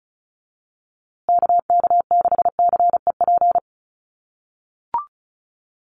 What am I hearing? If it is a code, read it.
Morse code, heard as a keyed tone, about 34 wpm, 700 Hz. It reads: KK6CEP
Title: USA Call Signs Repeated 3x 35wpm